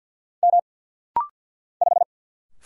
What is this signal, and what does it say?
Morse code, heard as a keyed tone, about 49 wpm, 700 Hz. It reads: M 5
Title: Single Letter-Number - Rapid-Fire 50wpm